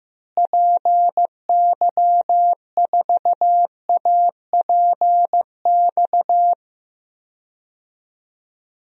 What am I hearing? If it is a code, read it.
Morse code, heard as a keyed tone, about 15 wpm, 700 Hz. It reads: PY4APX
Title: DXCC-100 Call Signs - 6 Characters - Repeated 3x - 15wpm